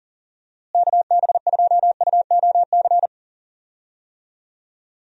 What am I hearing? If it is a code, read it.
Morse code, heard as a keyed tone, about 38 wpm, 700 Hz. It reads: KB2UOC
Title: USA Call Signs - 6 Characters - Repeated 2x - 40wpm